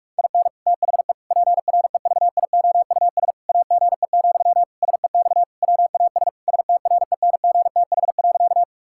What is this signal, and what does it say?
Morse code, heard as a keyed tone, about 46 wpm, 700 Hz. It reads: IN THE PREVIOUS AGE, SEX WAS STRENGTH.
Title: Sentences from Top 1600 Words 45wpm